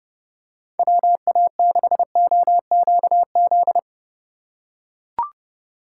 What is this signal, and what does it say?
Morse code, heard as a keyed tone, about 30 wpm, 700 Hz. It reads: WA6OQZ
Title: USA Call Signs - 2x Character Spacing - 30wpm